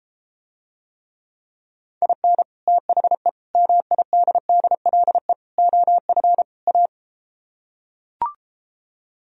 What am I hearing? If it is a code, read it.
Morse code, heard as a keyed tone, about 33 wpm, 700 Hz. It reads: IN THE MIDDLE OF A